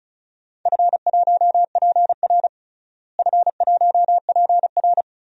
Morse code, heard as a keyed tone, about 36 wpm, 700 Hz. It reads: F1PR F1PR
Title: DXCC-100 Call Signs - 4 Characters - Repeated 3x - 35wpm